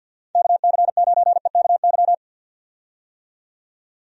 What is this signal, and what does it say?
Morse code, heard as a keyed tone, about 50 wpm, 700 Hz. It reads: KC9EKY
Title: USA Call Signs - 6 Characters - 50wpm